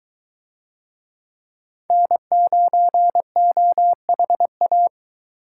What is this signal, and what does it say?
Morse code, heard as a keyed tone, about 22 wpm, 700 Hz. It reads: N9OHA